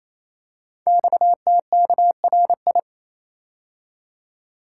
Morse code, heard as a keyed tone, about 28 wpm, 700 Hz. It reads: XTKRI
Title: Code Groups - 5 Letters - 28wpm